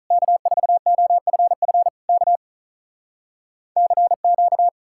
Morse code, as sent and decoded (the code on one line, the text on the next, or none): -.- ....- --- ..-. ..-. / -.- / -.-. --.-
K4OFF K CQ